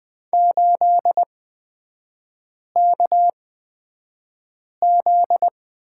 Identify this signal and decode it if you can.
Morse code, heard as a keyed tone, about 20 wpm, spaced as for 5 wpm, 700 Hz. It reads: 8KZ
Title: 243 Lesson 39e - Letter z - Three Characters 20wpm